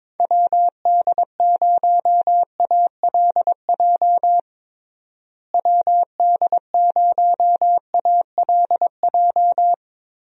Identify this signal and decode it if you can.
Morse code, heard as a keyed tone, about 22 wpm, 700 Hz. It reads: WD0ALJ WD0ALJ
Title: USA Call Signs - 6 Characters - Repeated 3x - 22wpm